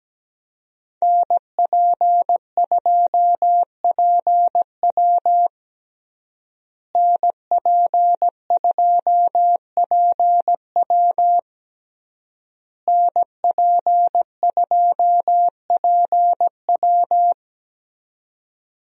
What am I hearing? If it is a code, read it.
Morse code, heard as a keyed tone, about 17 wpm, 700 Hz. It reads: NP2PW NP2PW NP2PW
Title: USA Call Signs - 5 Characters - Repeated 3x - 17wpm